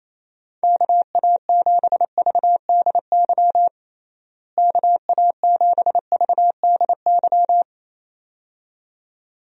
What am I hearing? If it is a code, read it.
Morse code, heard as a keyed tone, about 28 wpm, 700 Hz. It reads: KA7VDY KA7VDY
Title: USA Call Signs - 6 Characters - Repeated 2x - 28wpm